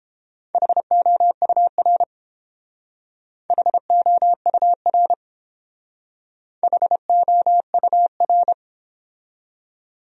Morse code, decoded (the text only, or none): HOUR HOUR HOUR